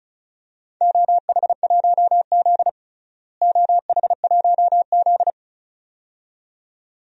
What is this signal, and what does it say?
Morse code, heard as a keyed tone, about 34 wpm, 700 Hz. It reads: OH1Z OH1Z
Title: DXCC-100 Call Signs - 4 Characters - Repeated 2x - 35wpm